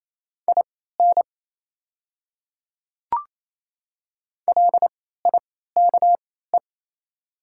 Morse code, heard as a keyed tone, about 27 wpm, spaced as for 16 wpm, 690 Hz. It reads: IN LIKE
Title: 3x Character Spacing - Top 100 Words - 28wpm